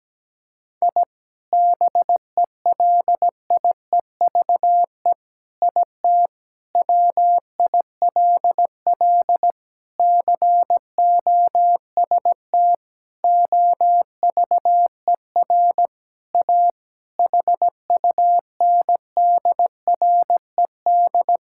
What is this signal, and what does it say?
Morse code, heard as a keyed tone, about 17 wpm, 700 Hz. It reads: I BELIEVE IT WILL COST OVER A HUNDRED
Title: Sentences from Top 1000 Words - Review Part 5 - 17wpm